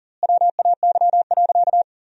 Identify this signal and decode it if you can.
Morse code, heard as a keyed tone, about 40 wpm, 700 Hz. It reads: WAY.
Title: 4x Word-Spacing - Sentences from Top 300 Words - 40wpm